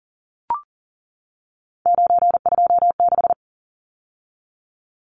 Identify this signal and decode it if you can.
Morse code, heard as a keyed tone, about 40 wpm, 700 Hz. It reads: 926
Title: Numbers - 3-Digit 40wpm